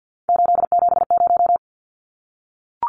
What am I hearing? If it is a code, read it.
Morse code, heard as a keyed tone, about 50 wpm, 700 Hz. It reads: 870
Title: Numbers - 3-Digit 50wpm